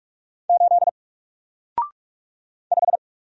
Morse code, heard as a keyed tone, about 44 wpm, 700 Hz. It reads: 8 5
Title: Single Letter-Number Alt 45wpm